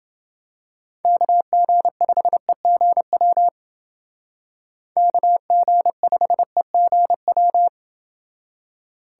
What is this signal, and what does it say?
Morse code, heard as a keyed tone, about 28 wpm, 700 Hz. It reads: KG5EGW KG5EGW